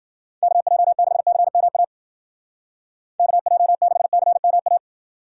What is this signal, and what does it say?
Morse code, heard as a keyed tone, about 50 wpm, 700 Hz. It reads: KJ6CMA KJ6CMA